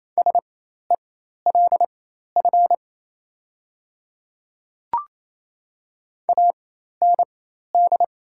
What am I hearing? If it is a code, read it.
Morse code, heard as a keyed tone, about 27 wpm, spaced as for 13 wpm, 700 Hz. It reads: SELF AND
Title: 4x Character Spacing - Top 200 Words - 28wpm